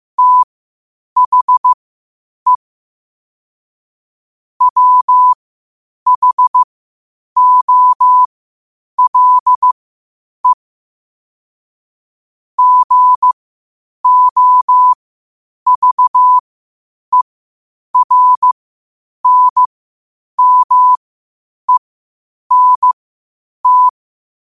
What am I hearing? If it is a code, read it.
Morse code, heard as a keyed tone, about 15 wpm, spaced as for 8 wpm, 1000 Hz. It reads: THE WHOLE GOVERNMENT
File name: qotdfiles / 8wpm